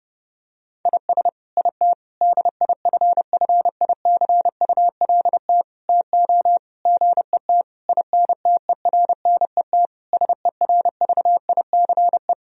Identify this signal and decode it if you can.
Morse code, heard as a keyed tone, about 30 wpm, 700 Hz. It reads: IS IT DIFFICULT TO GET INTERNET SERVICE
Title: Sentences from Top 1000 Words - Review Part 4 - 30wpm